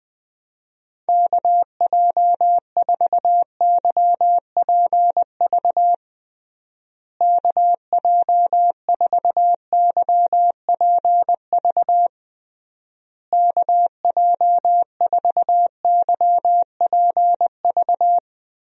Morse code, heard as a keyed tone, about 20 wpm, 700 Hz. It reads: KJ4YPV KJ4YPV KJ4YPV
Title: USA Call Signs - 6 Characters - Repeated 3x - 20wpm